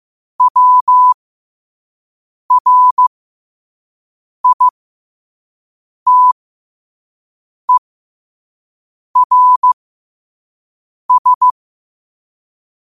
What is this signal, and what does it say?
Morse code, heard as a keyed tone, about 15 wpm, spaced as for 5 wpm, 1000 Hz. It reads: WRITERS
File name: qotdfiles / 5wpm